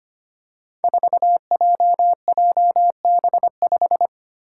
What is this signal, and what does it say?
Morse code, heard as a keyed tone, about 25 wpm, 700 Hz. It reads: 4JJB5
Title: Code Groups - 5 Letter:Numbers - 25wpm